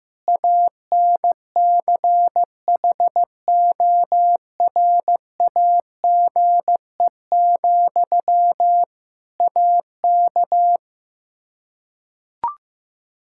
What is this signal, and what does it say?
Morse code, heard as a keyed tone, about 15 wpm, 700 Hz. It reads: ANCHORAGE, AK
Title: K1USN SST QSO Practice 15wpm